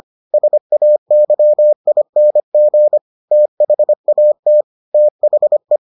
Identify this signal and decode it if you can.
Morse code, heard as a keyed tone, about 25 wpm, 600 Hz. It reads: SAYING THAT THE